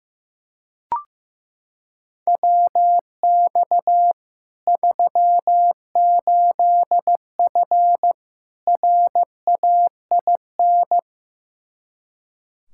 Morse code, heard as a keyed tone, about 15 wpm, 700 Hz. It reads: WX 38F RAIN
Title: Practice QSO - Exchange 2 - Info - 15wpm